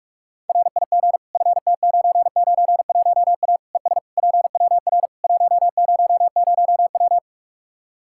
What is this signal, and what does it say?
Morse code, heard as a keyed tone, about 41 wpm, 700 Hz. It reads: RIG FT991A ES PWR 100W
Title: Practice QSO - Exchange 2 - Info - 45wpm